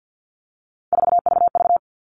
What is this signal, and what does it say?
Morse code, heard as a keyed tone, about 50 wpm, 700 Hz. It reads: 4VV
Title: All Cut Numbers - 3-Digit 50wpm